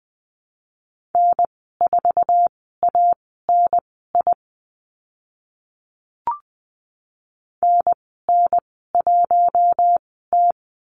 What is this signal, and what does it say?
Morse code, heard as a keyed tone, about 20 wpm, spaced as for 14 wpm, 700 Hz. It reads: N4ANI NN1T
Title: 062 Lesson 11g - Letter h - Call Signs 20wpm